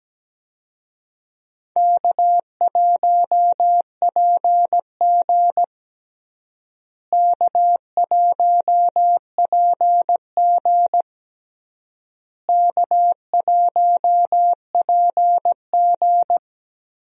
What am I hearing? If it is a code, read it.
Morse code, heard as a keyed tone, about 17 wpm, 700 Hz. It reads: K1PG K1PG K1PG
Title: USA Call Signs - 4 Characters - Repeated 3x - 17wpm